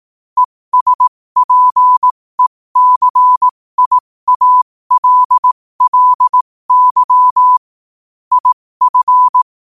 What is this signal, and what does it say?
Morse code, heard as a keyed tone, about 18 wpm, spaced as for 15 wpm, 1000 Hz. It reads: ESPECIALLY IF
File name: qotdfiles / 15wpm